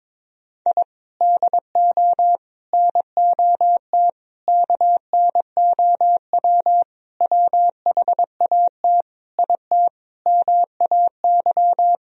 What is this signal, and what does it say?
Morse code, heard as a keyed tone, about 22 wpm, 700 Hz. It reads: I DO NOT KNOW WHAT IT MAY
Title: Sentences from Top 1000 Words - Review Part 2 - 22wpm